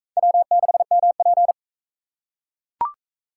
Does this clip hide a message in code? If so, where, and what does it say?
Yes: Morse code from 0.2 to 1.5 s, heard as a keyed tone, about 36 wpm, 700 Hz. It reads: W6MP